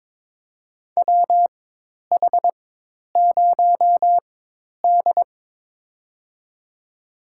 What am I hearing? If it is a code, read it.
Morse code, heard as a keyed tone, about 22 wpm, spaced as for 10 wpm, 700 Hz. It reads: WH0D
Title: USA Call Signs - 4x Character Spacing - 22wpm